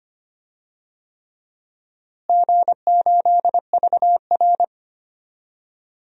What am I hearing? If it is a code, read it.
Morse code, heard as a keyed tone, about 25 wpm, 700 Hz. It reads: G8VR
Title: DXCC-100 Call Signs - 4 Characters - Repeated 3x - 25wpm